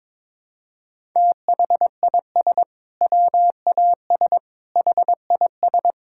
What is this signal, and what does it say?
Morse code, heard as a keyed tone, about 22 wpm, 700 Hz. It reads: THIS WAS HIS
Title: Sentences from Top 1000 Words - Review Part 6 - 22wpm